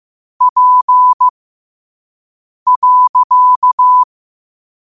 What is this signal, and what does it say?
Morse code, heard as a keyed tone, about 15 wpm, spaced as for 5 wpm, 1000 Hz. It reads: P.
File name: qotdfiles / 5wpm